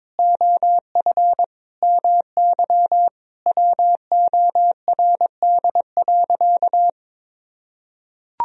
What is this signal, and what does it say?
Morse code, heard as a keyed tone, about 22 wpm, 700 Hz. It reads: OF MY WORD.
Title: Sentences from Top 1000 Words - Review Part 2 - 22wpm